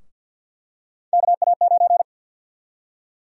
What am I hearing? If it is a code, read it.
Morse code, heard as a keyed tone, about 51 wpm, 700 Hz. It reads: KA9
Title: USA Call Sign Prefixes - 50wpm